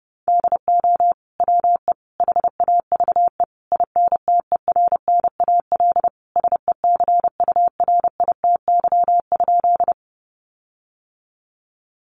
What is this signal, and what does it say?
Morse code, heard as a keyed tone, about 30 wpm, 700 Hz. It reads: DO WE HAVE INTERNAL SECURITY?
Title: Sentences from Top 1400 Words 30wpm